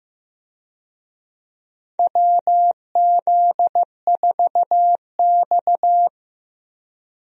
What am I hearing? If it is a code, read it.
Morse code, heard as a keyed tone, about 15 wpm, 700 Hz. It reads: WZ4X
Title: USA Call Signs - 4 Characters - Repeated 3x - 15wpm